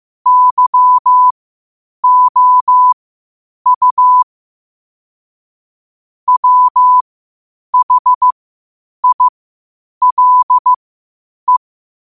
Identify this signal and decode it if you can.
Morse code, heard as a keyed tone, about 15 wpm, spaced as for 8 wpm, 1000 Hz. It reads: YOU WHILE
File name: qotdfiles / 8wpm